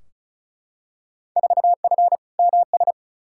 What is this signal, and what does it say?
Morse code, heard as a keyed tone, about 36 wpm, 700 Hz. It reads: 4F MS
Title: Field Day Exchanges Encore - 35wpm